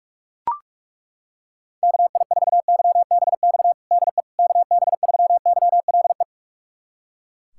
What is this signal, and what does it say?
Morse code, heard as a keyed tone, about 42 wpm, 700 Hz. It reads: KI4YBX DE KB3YLE
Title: USA Callsign DE Callsign - 6 Characters - 45wpm